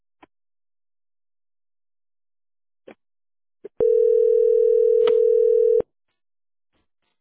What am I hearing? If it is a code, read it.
Ringback tone (North America), 440 + 480 Hz.